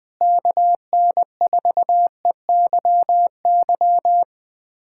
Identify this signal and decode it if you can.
Morse code, heard as a keyed tone, about 21 wpm, 700 Hz. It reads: KN4EYY
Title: USA Callsign DE Callsign 20wpm